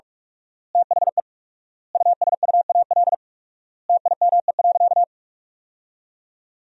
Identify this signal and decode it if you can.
Morse code, heard as a keyed tone, about 41 wpm, 700 Hz. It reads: THE USUAL TIME.
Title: 4x Word-Spacing - Sentences from Top 400 Words - 45wpm